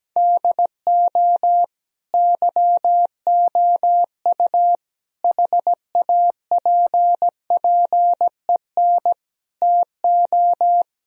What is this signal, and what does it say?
Morse code, heard as a keyed tone, about 17 wpm, 700 Hz. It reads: DO YOU HAPPEN TO
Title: Sentences from Top 1000 Words - Review Part 2 - 17wpm